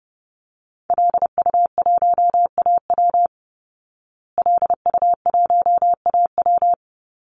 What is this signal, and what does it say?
Morse code, heard as a keyed tone, about 30 wpm, 700 Hz. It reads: LU1AW LU1AW
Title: Call Signs Repeated 2x - 30wpm